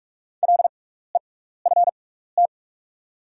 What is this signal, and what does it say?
Morse code, heard as a keyed tone, about 44 wpm, spaced as for 15 wpm, 690 Hz. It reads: LEFT
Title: 6x Character Spacing - Top 300 Words - 45wpm